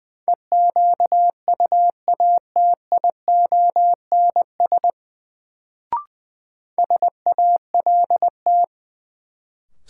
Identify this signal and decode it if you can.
Morse code, heard as a keyed tone, about 21 wpm, 700 Hz. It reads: EQUATIONS SALT
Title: Sets of 5 Words - New Word Drill - 20wpm